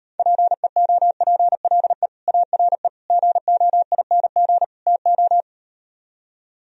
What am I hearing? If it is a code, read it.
Morse code, heard as a keyed tone, about 39 wpm, 700 Hz. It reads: PEOPLE ARE GOING TO